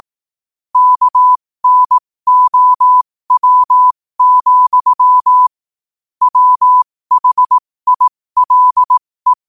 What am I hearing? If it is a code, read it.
Morse code, heard as a keyed tone, about 18 wpm, spaced as for 15 wpm, 1000 Hz. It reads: KNOW, WHILE
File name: qotdfiles / 15wpm